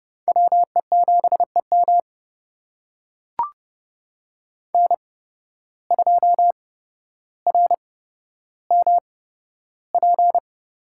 Morse code, heard as a keyed tone, about 30 wpm, spaced as for 13 wpm, 700 Hz. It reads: WE7EM N2RMP
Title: USA Call Signs - 8x Character Spacing - 30wpm